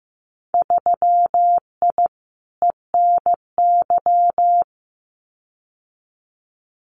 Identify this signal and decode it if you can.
Morse code, heard as a keyed tone, about 15 wpm, 700 Hz. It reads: 3I ENY
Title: Winter Field Day Exchanges Encore 15wpm